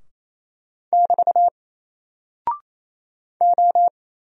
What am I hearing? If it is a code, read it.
Morse code, heard as a keyed tone, about 28 wpm, 700 Hz. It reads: <BT> O
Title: Single Character (Letter:Number:Punctuation:Prosign) 28wpm